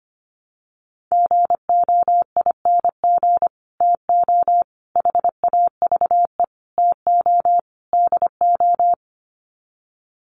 Morse code, heard as a keyed tone, about 25 wpm, 700 Hz. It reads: GOING TO HAVE TO DO
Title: Sets of 5 Words Encore 25wpm